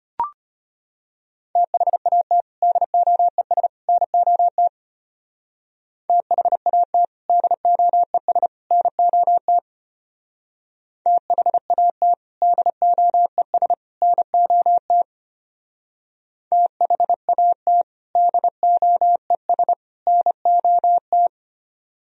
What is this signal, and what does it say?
Morse code, heard as a keyed tone, about 30 wpm, 700 Hz. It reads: THAT DOES NOT THAT DOES NOT THAT DOES NOT THAT DOES NOT
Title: Speed-Racing - Sets of 3 Words 25wpm